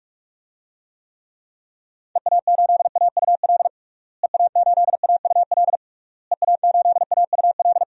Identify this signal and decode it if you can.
Morse code, heard as a keyed tone, about 42 wpm, 700 Hz. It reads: EA8AUL EA8AUL EA8AUL
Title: DXCC-100 Call Signs - 6 Characters - Repeated 3x - 45wpm